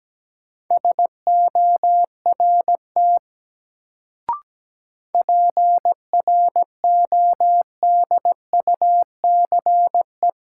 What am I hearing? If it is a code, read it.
Morse code, heard as a keyed tone, about 17 wpm, 700 Hz. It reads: SORT PRODUCE
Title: Sentences from Top 1000 Words - Review Part 3 - New Word Drill - 17wpm